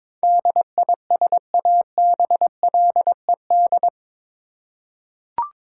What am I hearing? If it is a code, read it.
Morse code, heard as a keyed tone, about 21 wpm, 700 Hz. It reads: DISABLED
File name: practice / 22wpm / Eight Letter Words - 22wpm